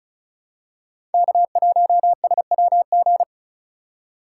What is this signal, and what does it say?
Morse code, heard as a keyed tone, about 34 wpm, 700 Hz. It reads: K1SWG
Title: USA Call Signs - 5 Characters - 3x Character Spacing - 35wpm